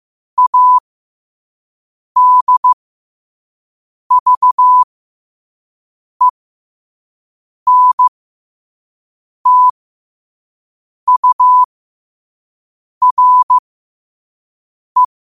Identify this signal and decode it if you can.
Morse code, heard as a keyed tone, about 15 wpm, spaced as for 5 wpm, 1000 Hz. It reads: ADVENTURE
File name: qotdfiles / 5wpm